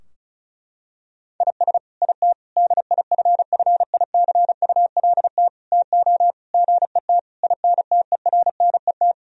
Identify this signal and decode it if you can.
Morse code, heard as a keyed tone, about 36 wpm, 700 Hz. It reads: IS IT DIFFICULT TO GET INTERNET
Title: Sentences from Top 1000 Words - Review Part 4 - 35wpm